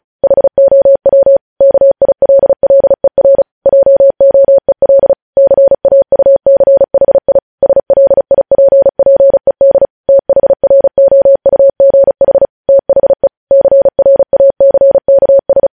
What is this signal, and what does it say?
Morse code, heard as a keyed tone, about 36 wpm, 550 Hz. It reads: HOW KILLER JOEL CAUCHI SLIPPED THROUGH THE CRACKS
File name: podcast / News headlines-1x-35x35